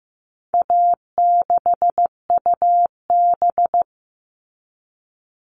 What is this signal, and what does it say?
Morse code, heard as a keyed tone, about 15 wpm, 700 Hz. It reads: A6UB
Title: All Cut Numbers - 4-Digit 15wpm